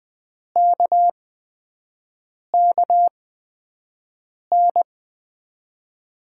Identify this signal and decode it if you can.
Morse code, heard as a keyed tone, about 20 wpm, spaced as for 5 wpm, 700 Hz. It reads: KKN
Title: USA Call Signs - 6 Characters - 8x Character Spacing - 20wpm